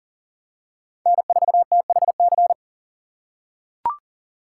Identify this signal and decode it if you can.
Morse code, heard as a keyed tone, about 38 wpm, 700 Hz. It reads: N4THC
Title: USA Call Signs - 5 Characters - 3x Character Spacing - 40wpm